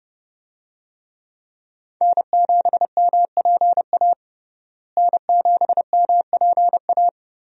Morse code, heard as a keyed tone, about 32 wpm, 700 Hz. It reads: N7MPA N7MPA
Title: USA Call Signs - 5 Characters - Repeated 3x - 30wpm